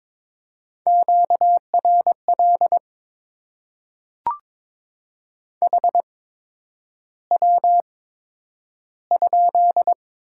Morse code, heard as a keyed tone, about 22 wpm, spaced as for 9 wpm, 700 Hz. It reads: QRL HW?
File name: Top 20 QSO Elements - 8x Character Spacing - 22wpm